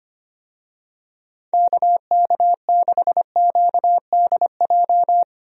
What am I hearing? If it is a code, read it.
Morse code, heard as a keyed tone, about 25 wpm, 700 Hz. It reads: KK6QDJ
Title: USA Call Signs - 6 Characters - Repeated 3x - 25wpm